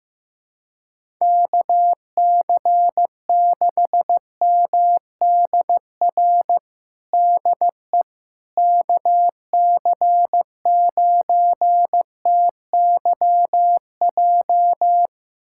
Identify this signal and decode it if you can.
Morse code, heard as a keyed tone, about 15 wpm, 700 Hz. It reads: KC6MDR DE KC9TYJ
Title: USA Callsign DE Callsign - 6 Characters - 15wpm